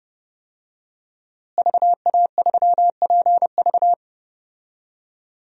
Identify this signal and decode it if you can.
Morse code, heard as a keyed tone, about 30 wpm, 700 Hz. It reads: VA3PV
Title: DXCC-100 Call Signs - 5 Characters - Repeated 3x - 30wpm